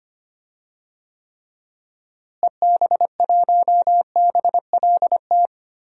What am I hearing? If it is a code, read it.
Morse code, heard as a keyed tone, about 25 wpm, 700 Hz. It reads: EB1BLT
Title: DXCC-100 Call Signs - 6 Characters - Repeated 3x - 25wpm